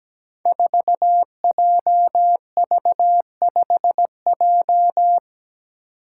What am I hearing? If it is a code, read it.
Morse code, heard as a keyed tone, about 17 wpm, 700 Hz. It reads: 4JV5J
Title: Five Challenging Letter-Number Combinations 17wpm